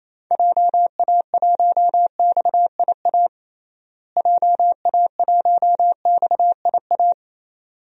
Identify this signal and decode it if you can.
Morse code, heard as a keyed tone, about 28 wpm, 700 Hz. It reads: JA1XIA JA1XIA
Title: DXCC-100 Call Signs - 6 Characters - Repeated 2x - 28wpm